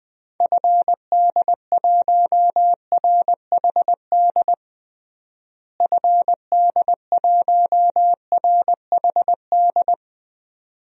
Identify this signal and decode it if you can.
Morse code, heard as a keyed tone, about 20 wpm, 700 Hz. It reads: FD1RHD FD1RHD
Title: DXCC-100 Call Signs - 6 Characters - Repeated 2x - 20wpm